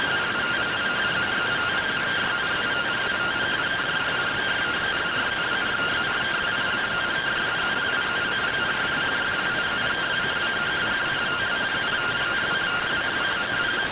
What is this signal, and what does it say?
Two-tone teleprinter signal, about 170 Hz shift.